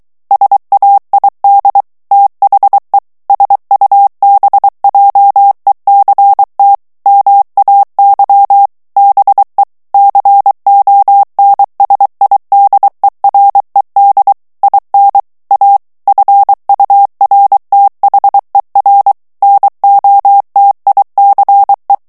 Morse code, heard as a keyed tone, about 23 wpm, 800 Hz. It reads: SAID THE SUBJECT MAY BE CONSIDERED IN A FURTHER NOTICE
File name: ANGLAIS texte clair 3 vitesse 22 mots minute